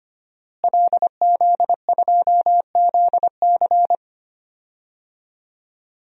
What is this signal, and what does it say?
Morse code, heard as a keyed tone, about 25 wpm, 700 Hz. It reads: LZ2ZC
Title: DXCC-100 Call Signs - 5 Characters - Repeated 2x - 25wpm